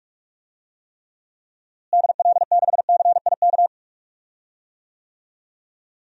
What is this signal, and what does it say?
Morse code, heard as a keyed tone, about 42 wpm, 700 Hz. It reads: DL6CIK
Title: DXCC-100 Call Signs - 6 Characters - Repeated 3x - 45wpm